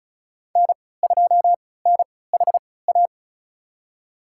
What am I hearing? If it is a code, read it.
Morse code, heard as a keyed tone, about 34 wpm, spaced as for 20 wpm, 690 Hz. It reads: N2NHA
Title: USA Call Signs - 3x Character Spacing - 35wpm